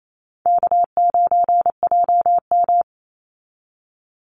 Morse code, heard as a keyed tone, about 28 wpm, 700 Hz. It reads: K9JM
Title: Easy Call Signs - 28wpm